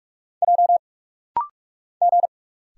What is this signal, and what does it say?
Morse code, heard as a keyed tone, about 46 wpm, 700 Hz. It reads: J G
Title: Single Letters - Rapid-Fire 45wpm